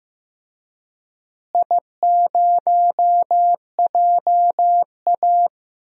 Morse code, heard as a keyed tone, about 15 wpm, 700 Hz. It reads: I0JA
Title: DXCC-100 Call Signs - 4 Characters - Repeated 3x - 15wpm